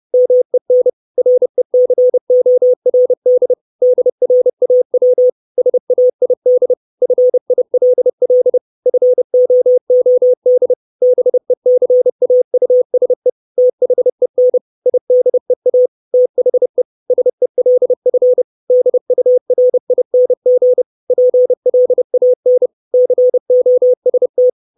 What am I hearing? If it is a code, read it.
Morse code, heard as a keyed tone, about 30 wpm, 500 Hz. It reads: MEN RECORD DRAW SAID FILL FOOD BECAUSE THEN IDEA THE SELF DURING PLAN COST
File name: 632 Words 30WPM